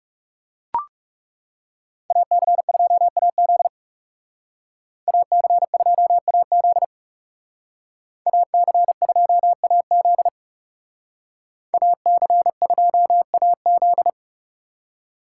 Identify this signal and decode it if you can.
Morse code, heard as a keyed tone, about 38 wpm, 700 Hz. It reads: AC2AZ AC2AZ AC2AZ AC2AZ